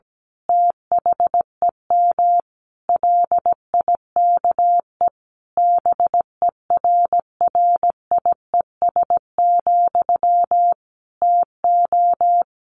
Morse code, heard as a keyed tone, about 17 wpm, 700 Hz. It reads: THEM LIKE BERRIES, TO
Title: A Sound Of Thunder 17wpm